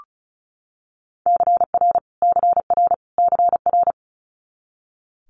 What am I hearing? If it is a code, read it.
Morse code, heard as a keyed tone, about 36 wpm, 700 Hz. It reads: CR CR CR
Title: Multi-letter Phonemes Repeated 3x 35wpm